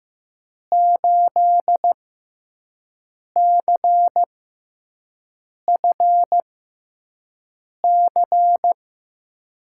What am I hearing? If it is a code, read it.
Morse code, heard as a keyed tone, about 15 wpm, spaced as for 5 wpm, 700 Hz. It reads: 8CFC
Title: USA Call Signs - 5 Characters - 6x Character Spacing - 15wpm